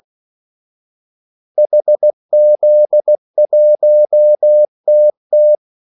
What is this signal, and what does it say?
Morse code, heard as a keyed tone, about 16 wpm, 600 Hz. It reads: HZ1TT
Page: Kaldesignaler 16 wpm